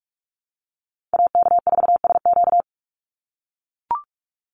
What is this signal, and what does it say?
Morse code, heard as a keyed tone, about 44 wpm, 700 Hz. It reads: AK4SQ
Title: Call Signs - 45wpm